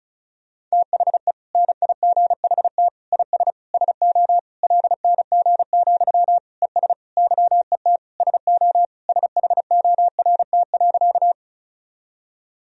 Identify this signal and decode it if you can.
Morse code, heard as a keyed tone, about 36 wpm, 700 Hz. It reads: THE NIGHT IS SO LNG, ES YET SO SHORT.
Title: Sentences from Top 500 Words and Abbreviations 35wpm